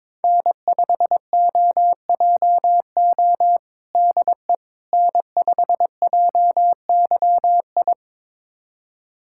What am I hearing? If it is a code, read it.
Morse code, heard as a keyed tone, about 22 wpm, 700 Hz. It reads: N5OJO DE N5JYI
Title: USA Callsign DE Callsign - 5 Characters - 22wpm